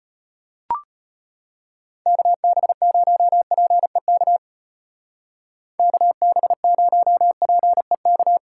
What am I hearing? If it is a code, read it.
Morse code, heard as a keyed tone, about 31 wpm, 700 Hz. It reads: KB0PEK KB0PEK
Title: Speed Racing - USA Call Signs 25wpm